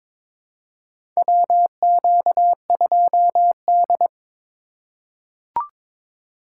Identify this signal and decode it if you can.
Morse code, heard as a keyed tone, about 22 wpm, 700 Hz. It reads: WQ2D
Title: USA Call Signs - 4x Character Spacing - 22wpm